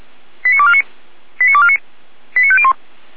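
Five-tone selective call (ZVEI), heard as quick runs of five tones, tones of about 70 ms. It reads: 89249 89249 89501
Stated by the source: ZVEI five-tone selective call received by SDR, logged by feuerwehr-orlamuende.org, a fire brigade.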